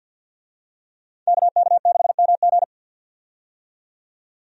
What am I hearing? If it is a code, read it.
Morse code, heard as a keyed tone, about 50 wpm, 700 Hz. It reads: KK6MG
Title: USA Call Signs - 5 Characters - 50wpm